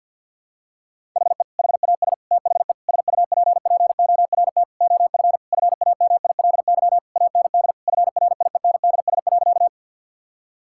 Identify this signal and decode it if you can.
Morse code, heard as a keyed tone, about 50 wpm, 700 Hz. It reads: HE HAS THE SUPPORT OF FAMILY AND FRIENDS.
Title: Sentences from Top 1000 Words - Review Part 5 - 50wpm